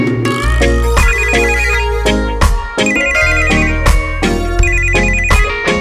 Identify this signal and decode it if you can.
There is no code.